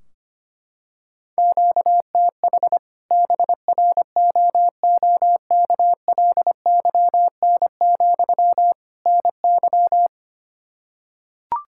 Morse code, heard as a keyed tone, about 25 wpm, 700 Hz. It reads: QTH BROOKLYN, NY
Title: Practice QSO - Exchange 1 - Beginning - 25wpm